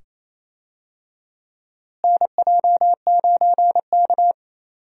Morse code, heard as a keyed tone, about 27 wpm, 700 Hz. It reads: NJ9K
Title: USA Call Signs - 4 Characters - Repeated 2x - 28wpm